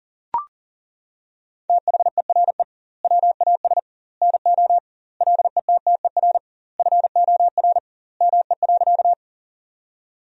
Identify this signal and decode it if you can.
Morse code, heard as a keyed tone, about 38 wpm, 700 Hz. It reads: THERE WAS NO LETTER FOR ME.
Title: 2x Word-Spacing - Sentences from Top 300 Words - 40wpm